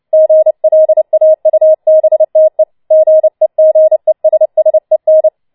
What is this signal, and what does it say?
Morse code, heard as a keyed tone, about 29 wpm, 630 Hz. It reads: GLAUBTE GEGESSEN
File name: Schneewittchen Teil3 30wpm